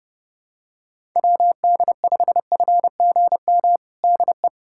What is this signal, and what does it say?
Morse code, heard as a keyed tone, about 30 wpm, 700 Hz. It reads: WD5FGM DE
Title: USA Callsign DE Callsign - 6 Characters - 30wpm